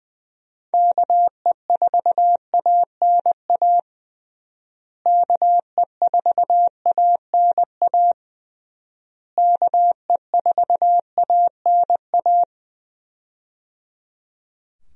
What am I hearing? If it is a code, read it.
Morse code, heard as a keyed tone, about 20 wpm, 700 Hz. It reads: KE4ANA KE4ANA KE4ANA
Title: USA Call Signs - 6 Characters - Repeated 3x - 20wpm